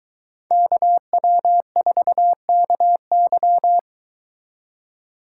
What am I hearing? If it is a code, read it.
Morse code, heard as a keyed tone, about 22 wpm, 700 Hz. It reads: KW4KY